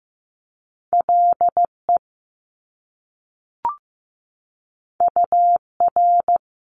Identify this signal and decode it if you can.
Morse code, heard as a keyed tone, about 15 wpm, 700 Hz. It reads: LE UR
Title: Top Two Letter Combinations in Words 15wpm